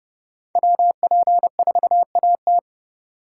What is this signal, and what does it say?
Morse code, heard as a keyed tone, about 36 wpm, 700 Hz. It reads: WP4AT
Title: Speed Racing - USA 2x2 Call Signs 30wpm